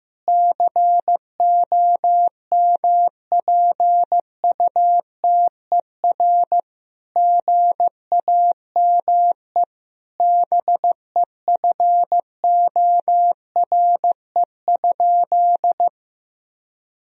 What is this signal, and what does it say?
Morse code, heard as a keyed tone, about 15 wpm, 700 Hz. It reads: COMPUTER GAME BEFORE?
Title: Sentences from Top 1000 Words - Review Part 3 - 15wpm